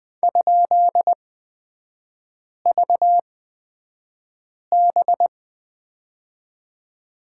Morse code, heard as a keyed tone, about 20 wpm, spaced as for 5 wpm, 700 Hz. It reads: ?VB
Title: 224 Lesson 36e - BT - Three Characters 20wpm